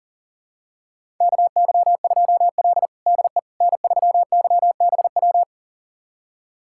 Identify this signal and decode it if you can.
Morse code, heard as a keyed tone, about 40 wpm, 700 Hz. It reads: KY2L DE N3YBW
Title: USA Callsign DE Callsign 40wpm